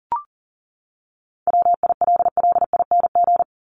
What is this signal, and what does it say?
Morse code, heard as a keyed tone, about 40 wpm, 700 Hz. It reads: WILLING
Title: Top 751-1000 Words 40wpm